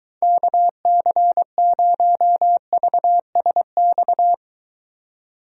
Morse code, heard as a keyed tone, about 23 wpm, 700 Hz. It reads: KC0VSX